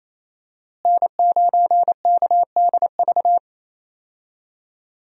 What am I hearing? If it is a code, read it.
Morse code, heard as a keyed tone, about 27 wpm, 700 Hz. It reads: N9KDV
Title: USA Call Signs - 5 Characters - Repeated 2x - 28wpm